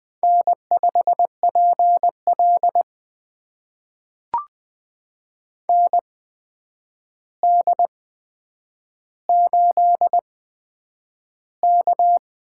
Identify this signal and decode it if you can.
Morse code, heard as a keyed tone, about 20 wpm, spaced as for 9 wpm, 700 Hz. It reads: N5PL ND8K
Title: USA Call Signs - 4 Characters - 8x Character Spacing - 20wpm